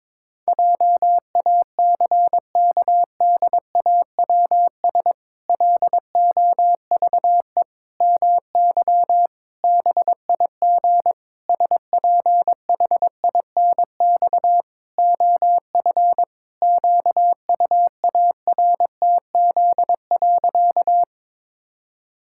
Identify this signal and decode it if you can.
Morse code, heard as a keyed tone, about 22 wpm, 700 Hz. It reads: JACKDAWS LOVE MY BIG SPHINX OF QUARTZ.
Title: Pangram Challenge v2 - 22wpm